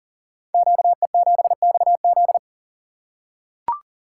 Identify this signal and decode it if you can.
Morse code, heard as a keyed tone, about 41 wpm, 700 Hz. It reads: QE7XZ
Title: Code Groups - 5 Letter:Numbers - 40wpm